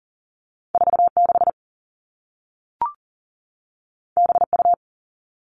Morse code, heard as a keyed tone, about 40 wpm, 700 Hz. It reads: 46 BU
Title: All Cut Numbers - 2-Digit 40wpm